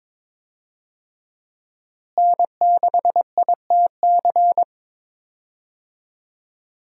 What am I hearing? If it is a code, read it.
Morse code, heard as a keyed tone, about 21 wpm, 700 Hz. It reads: N6ITC
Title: USA Call Signs - 5 Characters - Repeated 2x - 22wpm